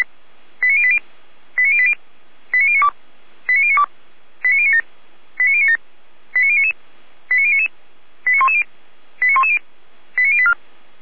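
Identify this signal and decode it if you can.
Five-tone selective call (ZVEI), heard as quick runs of five tones, tones of about 70 ms. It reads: 89080 89080 89092 89092 89097 89097 89099 89099 89110 89110 89084
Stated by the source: Five-tone ZVEI selective call 89073, received by SDR, fire brigade Orlamünde.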